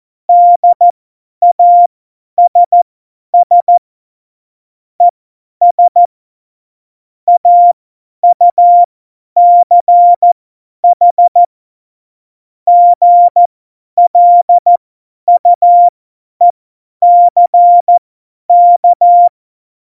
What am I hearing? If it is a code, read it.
Morse code, heard as a keyed tone, about 14 wpm, spaced as for 10 wpm, 700 Hz. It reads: DASS ES AUCH GLUECK
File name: Morsekurs / Witze / Witz 26. Januar 2021, 50 F70